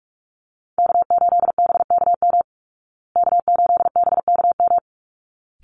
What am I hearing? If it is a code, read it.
Morse code, heard as a keyed tone, about 44 wpm, 700 Hz. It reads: K8BKM K8BKM
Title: Call Signs Repeated 3x - 45wpm